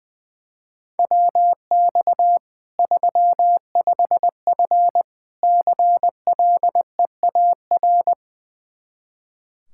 Morse code, heard as a keyed tone, about 20 wpm, 700 Hz. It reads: WX 35F CLEAR
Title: K1USN SST QSO Practice 20wpm